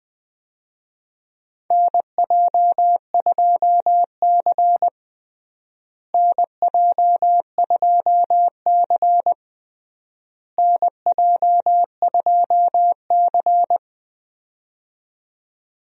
Morse code, heard as a keyed tone, about 20 wpm, 690 Hz. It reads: NJ2C NJ2C NJ2C
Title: USA Call Signs - 4 Characters - Repeated 3x - 20wpm